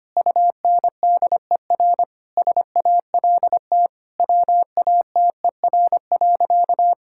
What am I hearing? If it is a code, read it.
Morse code, heard as a keyed tone, about 25 wpm, 700 Hz. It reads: UNDER SALT WATER.
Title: Sentences from Top 1000 Words - Review Part 5 - 25wpm